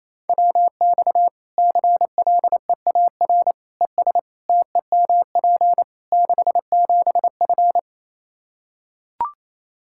Morse code, heard as a keyed tone, about 28 wpm, 700 Hz. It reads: WX CLEAR ES TEMP 67F
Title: K1USN SST QSO Practice 28wpm